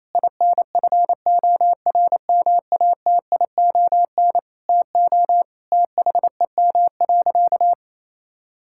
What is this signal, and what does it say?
Morse code, heard as a keyed tone, about 28 wpm, 700 Hz. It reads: INFORMATION TO THEM.
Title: Sentences from Top 1000 Words - Review Part 5 - 28wpm